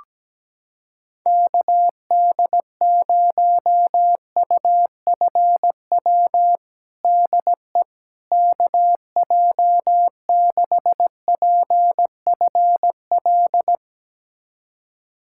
Morse code, heard as a keyed tone, about 17 wpm, 700 Hz. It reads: KD0UFW DE KJ6PFL
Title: USA Callsign DE Callsign - 6 Characters - 17wpm